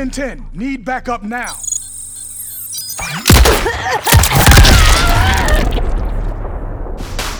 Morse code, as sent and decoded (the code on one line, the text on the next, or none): none